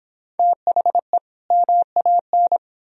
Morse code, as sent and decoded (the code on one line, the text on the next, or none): - .... / -- .- -.
THE MAN